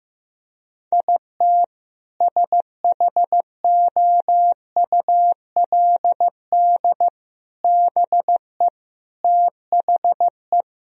Morse code, heard as a keyed tone, about 15 wpm, 700 Hz. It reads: IT SHOULD BE THE
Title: Sentences from Top 1000 Words - Review Part 1 - 15wpm